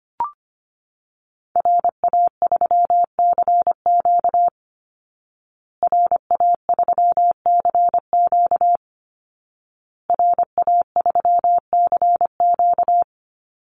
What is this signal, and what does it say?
Morse code, heard as a keyed tone, about 25 wpm, 700 Hz. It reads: RA3CQ RA3CQ RA3CQ
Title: Call Signs Repeated 2x - 25wpm